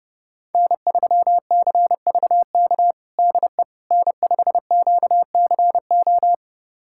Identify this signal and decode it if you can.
Morse code, heard as a keyed tone, about 30 wpm, 700 Hz. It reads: N3CVK DE N5QCO
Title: USA Callsign DE Callsign - 5 Characters - 30wpm